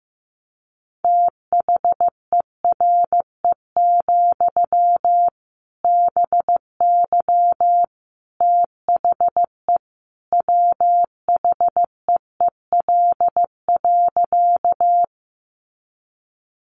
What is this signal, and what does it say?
Morse code, heard as a keyed tone, about 15 wpm, 700 Hz. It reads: THERE, BY THE WHEEL.
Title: Sentences from Top 500 Words 15wpm